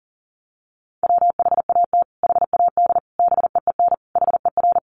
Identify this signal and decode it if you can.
Morse code, heard as a keyed tone, about 40 wpm, 700 Hz. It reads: WHAT HAD BEEN HER
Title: Sentences from Top 1500 Words 40wpm